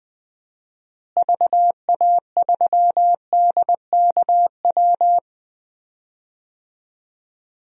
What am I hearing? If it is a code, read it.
Morse code, heard as a keyed tone, about 20 wpm, 700 Hz. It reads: VA3DKW
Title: DXCC-100 Call Signs - 6 Characters - Repeated 3x - 20wpm